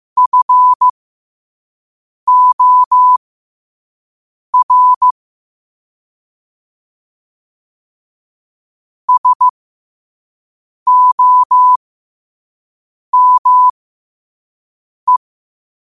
Morse code, heard as a keyed tone, about 15 wpm, spaced as for 5 wpm, 1000 Hz. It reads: FOR SOME
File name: qotdfiles / 5wpm